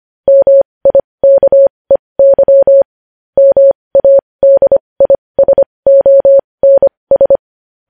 Morse code, heard as a keyed tone, about 25 wpm, spaced as for 20 wpm, 550 Hz. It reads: MIKEY MADISONS
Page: news-headlines-in-morse-code-at-20-wpm